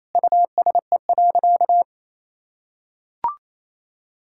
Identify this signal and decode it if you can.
Morse code, heard as a keyed tone, about 28 wpm, 700 Hz. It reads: USE.
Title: Sentences from Top 1000 Words - Review Part 6 - 28wpm